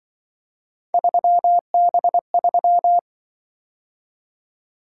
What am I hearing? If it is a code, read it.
Morse code, heard as a keyed tone, about 25 wpm, 700 Hz. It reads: 3B3
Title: Speed-Racing - Three Letter-Number Combinations - 20wpm